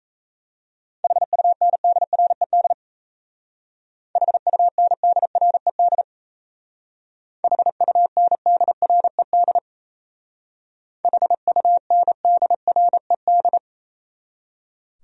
Morse code, heard as a keyed tone, about 33 wpm, 700 Hz. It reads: HUNDRED HUNDRED HUNDRED HUNDRED
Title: Speed-Racing Top 500 Words 28wpm